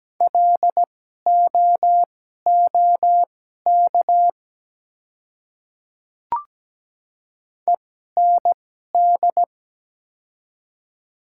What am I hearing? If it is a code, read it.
Morse code, heard as a keyed tone, about 17 wpm, spaced as for 12 wpm, 700 Hz. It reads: LOOK END
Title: 2x Character Spacing - Top 200 Words - 17wpm